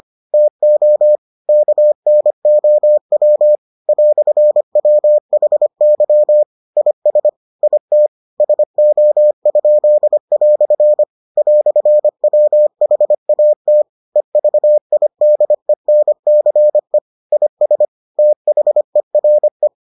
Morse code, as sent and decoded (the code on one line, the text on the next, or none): - --- / -.- -. --- .-- / .-..-. .-- .... -.-- / .. ... / .. - / ... --- ..--.. .-..-. / .-..-. .-- .... .- - / . ...- .. -.. -. -.-. / .. ... / - .... .-. .
TO KNOW "WHY IS IT SO?" "WHAT EVIDENCE IS THERE